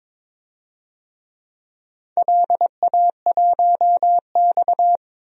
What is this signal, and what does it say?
Morse code, heard as a keyed tone, about 22 wpm, 700 Hz. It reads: LA1X
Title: DXCC-100 Call Signs - 4 Characters - 22wpm